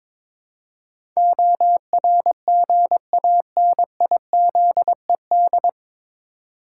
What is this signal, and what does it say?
Morse code, heard as a keyed tone, about 21 wpm, 700 Hz. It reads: ORGANIZED
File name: 9 Letter Words - First 3 Letters Given - 22wpm